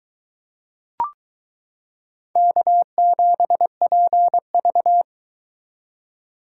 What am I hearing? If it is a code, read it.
Morse code, heard as a keyed tone, about 22 wpm, 700 Hz. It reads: K7PV